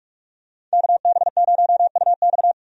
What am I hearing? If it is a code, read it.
Morse code, heard as a keyed tone, about 46 wpm, 700 Hz. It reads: KB0UX
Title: USA Call Signs - 5 Characters - 45wpm